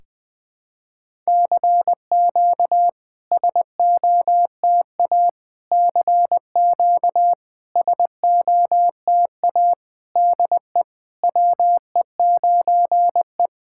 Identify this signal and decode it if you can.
Morse code, heard as a keyed tone, about 21 wpm, 700 Hz. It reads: CQ SOTA CQ SOTA DE WE9E
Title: SOTA QSO Practice 20wpm